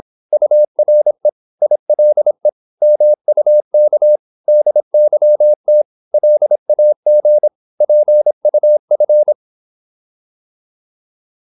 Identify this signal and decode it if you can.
Morse code, heard as a keyed tone, about 26 wpm, 600 Hz. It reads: URE ILE MUK DYT LAG PUF
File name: Korte ord DK 26wpm